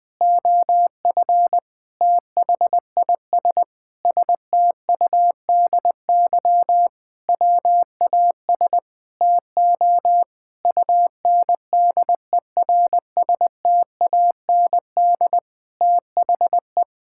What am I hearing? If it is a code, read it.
Morse code, heard as a keyed tone, about 21 wpm, 700 Hz. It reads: OF THIS STUDY WAS TO UNDERSTAND THE
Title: Sentences with 5-Word Phrases - 20wpm